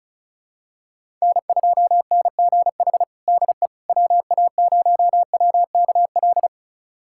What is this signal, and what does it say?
Morse code, heard as a keyed tone, about 34 wpm, 700 Hz. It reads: N2NGH DE WA0WKL
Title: USA Callsign DE Callsign 35wpm